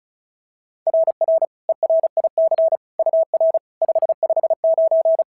Morse code, heard as a keyed tone, about 36 wpm, 690 Hz. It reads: RR ERIC UR 559
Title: SOTA QSO Practice 35wpm